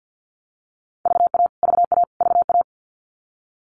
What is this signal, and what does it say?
Morse code, heard as a keyed tone, about 49 wpm, 700 Hz. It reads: VA VA VA
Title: US States Repeated 3x 50wpm